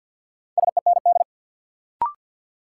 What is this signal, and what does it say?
Morse code, heard as a keyed tone, about 53 wpm, 700 Hz. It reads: SEND
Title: Sentences from Top 1500 Words - New Word Drill Alt - 50wpm